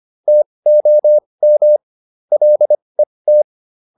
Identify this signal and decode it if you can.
Morse code, heard as a keyed tone, about 25 wpm, spaced as for 20 wpm, 600 Hz. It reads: TOM LET